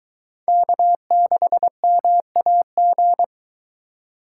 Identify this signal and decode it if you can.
Morse code, heard as a keyed tone, about 20 wpm, 700 Hz. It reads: K6MAG
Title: Speed Racing - USA 1x3 Call Signs 17wpm